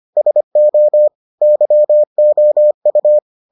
Morse code, heard as a keyed tone, about 25 wpm, 600 Hz. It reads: SO YOU